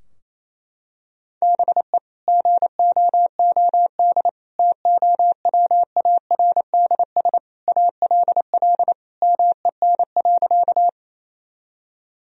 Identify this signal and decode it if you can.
Morse code, heard as a keyed tone, about 28 wpm, 700 Hz. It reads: BE GOOD TOWARDS ALL MEN.
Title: Sentences from Top 1000 Words - Review Part 2 - 28wpm